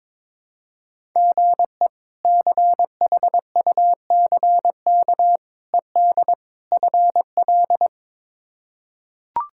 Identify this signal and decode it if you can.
Morse code, heard as a keyed tone, about 22 wpm, 700 Hz. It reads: GE CHUCK ED FL
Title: K1USN SST QSO Practice - 22wpm